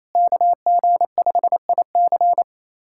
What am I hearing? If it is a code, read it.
Morse code, heard as a keyed tone, about 28 wpm, 700 Hz. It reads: KG5IC
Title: USA Callsign DE Callsign - 5 Characters - 28wpm